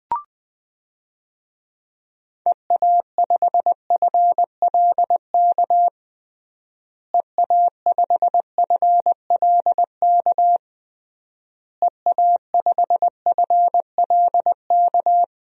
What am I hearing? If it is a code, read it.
Morse code, heard as a keyed tone, about 20 wpm, 700 Hz. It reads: EA5FLK EA5FLK EA5FLK
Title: DXCC-100 Call Signs - 6 Characters - Repeated 3x - 20wpm